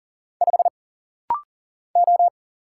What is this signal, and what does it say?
Morse code, heard as a keyed tone, about 40 wpm, 700 Hz. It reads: 5 O
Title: Single Letter-Number Alt - Rapid-Fire 40wpm